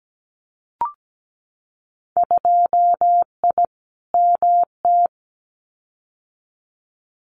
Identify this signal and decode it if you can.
Morse code, heard as a keyed tone, about 17 wpm, 700 Hz. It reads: 2I MT
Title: Winter Field Day Exchanges Encore 17wpm